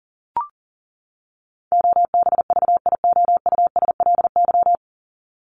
Morse code, heard as a keyed tone, about 40 wpm, 700 Hz. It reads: OBVIOUSLY
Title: Top 1501-1750 Words 40wpm